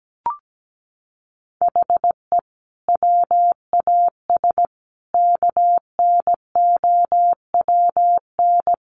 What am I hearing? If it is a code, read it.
Morse code, heard as a keyed tone, about 17 wpm, 700 Hz. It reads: HE WAS KNOWN
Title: Sentences from Top 1200 Words 17wpm